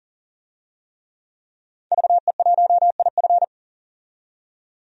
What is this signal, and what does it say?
Morse code, heard as a keyed tone, about 40 wpm, 700 Hz. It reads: VE1IF
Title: DXCC-100 Call Signs - 5 Characters - 40wpm